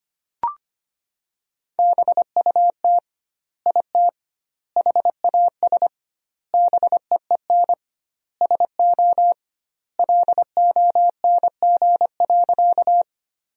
Morse code, heard as a keyed tone, about 25 wpm, 700 Hz. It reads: BUT IT HAS BEEN SO LONG.
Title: 2x Word-Spacing - Sentences from Top 100 Words - 25wpm